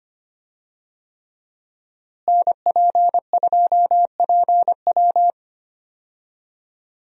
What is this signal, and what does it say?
Morse code, heard as a keyed tone, about 25 wpm, 700 Hz. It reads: NP2PW
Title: USA Call Signs - 5 Characters - Repeated 3x - 25wpm